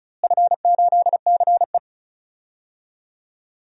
Morse code, heard as a keyed tone, about 34 wpm, 700 Hz. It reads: F8CE
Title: DXCC-100 Call Signs - 4 Characters - Repeated 3x - 35wpm